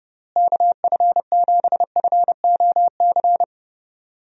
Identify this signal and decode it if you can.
Morse code, heard as a keyed tone, about 30 wpm, 700 Hz. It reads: KF7FOC
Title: USA Call Signs - 6 Characters - Repeated 2x - 30wpm